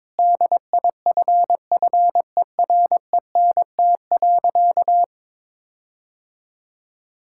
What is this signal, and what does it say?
Morse code, heard as a keyed tone, about 21 wpm, 700 Hz. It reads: DIFFERENT.
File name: practice / 22wpm / Sentences with 8-Word Phrases - 22wpm